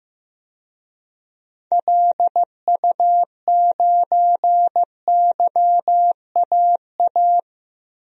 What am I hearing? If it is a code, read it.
Morse code, heard as a keyed tone, about 15 wpm, 700 Hz. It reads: LU9YAA
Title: DXCC-100 Call Signs - 6 Characters - 15wpm